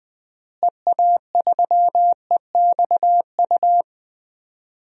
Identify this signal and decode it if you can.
Morse code, heard as a keyed tone, about 19 wpm, 700 Hz. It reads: EA3EXU
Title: DXCC-100 Call Signs - 6 Characters - Repeated 2x - 20wpm